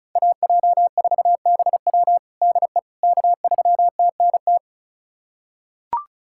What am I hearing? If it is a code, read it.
Morse code, heard as a keyed tone, about 36 wpm, 700 Hz. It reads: AJ4BW DE K3TNT
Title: USA Callsign DE Callsign - 5 Characters - 35wpm